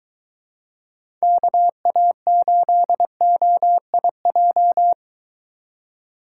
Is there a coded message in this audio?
Yes: Morse code from 1.2 to 4.9 s, heard as a keyed tone, about 21 wpm, 700 Hz. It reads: KA8OIJ